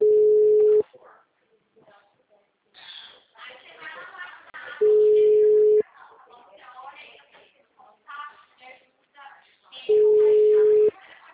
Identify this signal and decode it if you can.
Ringback tone (Europe, 425 Hz), with speech.